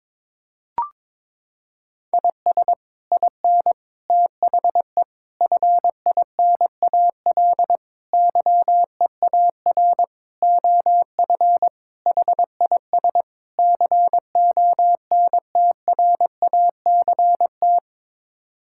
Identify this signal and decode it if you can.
Morse code, heard as a keyed tone, about 22 wpm, 700 Hz. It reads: IS IN THE FINAL YEAR OF HIS CONTRACT
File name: practice / 22wpm / Sets of 8 Words - 22wpm